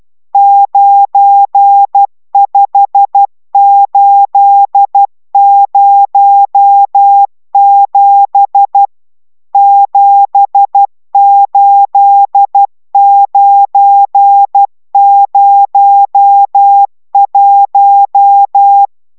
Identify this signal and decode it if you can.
Morse code, heard as a keyed tone, about 12 wpm, 800 Hz. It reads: 95807 78901